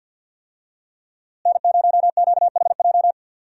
Morse code, heard as a keyed tone, about 52 wpm, 700 Hz. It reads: N0QHJ
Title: USA Call Signs - 5 Characters - Repeated 2x - 50wpm